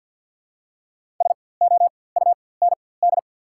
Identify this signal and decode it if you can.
Morse code, heard as a keyed tone, about 53 wpm, spaced as for 24 wpm, 700 Hz. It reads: SOUND
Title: 4x Character Spacing - Top 100 Words - 50wpm